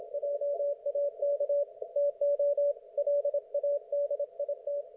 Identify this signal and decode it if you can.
Morse code, heard as a keyed tone, about 26 wpm, 560 Hz. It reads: JAK TO LADIT
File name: Archiv / CUC179 / ok1fhi cul179